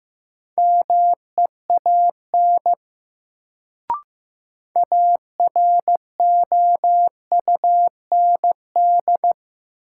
Morse code, heard as a keyed tone, about 15 wpm, 700 Hz. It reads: MEAN AROUND
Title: Sets of 2 Words - New Word Drill - 15wpm